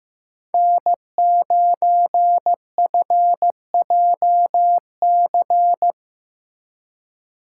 Morse code, heard as a keyed tone, about 18 wpm, 700 Hz. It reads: N9FJC
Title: Speed Racing - USA 1x3 Call Signs 15wpm